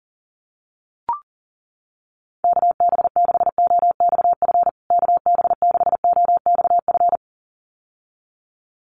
Morse code, heard as a keyed tone, about 40 wpm, 700 Hz. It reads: KB6OXF KB6OXF
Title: USA Call Signs Repeated 2x 40wpm